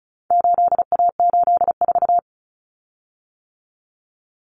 Morse code, heard as a keyed tone, about 34 wpm, 700 Hz. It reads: 8A84
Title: Common Cut Numbers - 4-Digit 35wpm